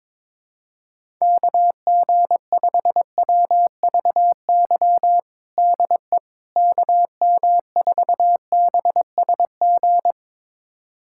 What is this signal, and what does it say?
Morse code, heard as a keyed tone, about 21 wpm, 700 Hz. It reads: KG5WVY DE KM4BSG
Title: USA Callsign DE Callsign - 6 Characters - 22wpm